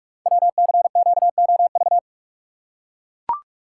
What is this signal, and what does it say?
Morse code, heard as a keyed tone, about 44 wpm, 700 Hz. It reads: WCQOV
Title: Code Groups - 5 Letters - 45wpm